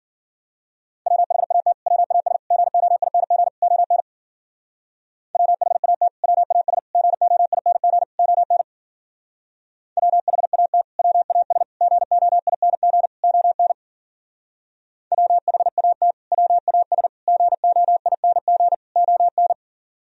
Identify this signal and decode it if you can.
Morse code, heard as a keyed tone, about 48 wpm, 700 Hz. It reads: WHAT WAS GOING ON WHAT WAS GOING ON WHAT WAS GOING ON WHAT WAS GOING ON
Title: Speed-Racing - Sets of 4 Words 40wpm